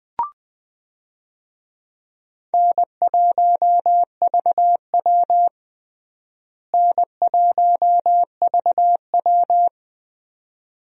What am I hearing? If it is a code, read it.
Morse code, heard as a keyed tone, about 20 wpm, 700 Hz. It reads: N1VW N1VW
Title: USA Call Signs - 4 Characters - Repeated 2x - 20wpm